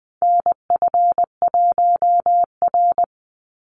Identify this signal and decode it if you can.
Morse code, heard as a keyed tone, about 20 wpm, 700 Hz. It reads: NF1R
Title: NA SOTA Chaser Callsigns 20wpm